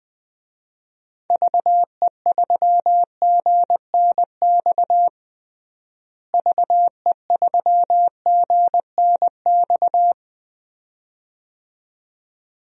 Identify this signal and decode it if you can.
Morse code, heard as a keyed tone, about 20 wpm, 700 Hz. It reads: VE3GNX VE3GNX
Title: DXCC-100 Call Signs - 6 Characters - Repeated 2x - 20wpm